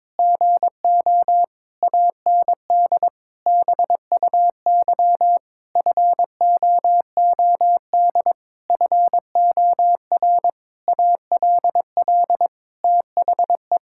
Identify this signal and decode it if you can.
Morse code, heard as a keyed tone, about 22 wpm, 700 Hz. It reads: GO AND BUY FOOD FOR ALL THE
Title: Sentences from Top 1600 Words 22wpm